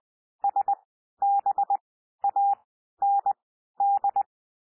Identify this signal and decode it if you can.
Morse code, heard as a keyed tone, about 20 wpm, spaced as for 12 wpm, 800 Hz. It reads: SBAND
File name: cwmp3 / P1 SOUND PASSAGE 20 wpm